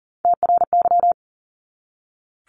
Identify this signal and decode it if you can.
Morse code, heard as a keyed tone, about 39 wpm, 700 Hz. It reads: TRY
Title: Top 200 Words 40wpm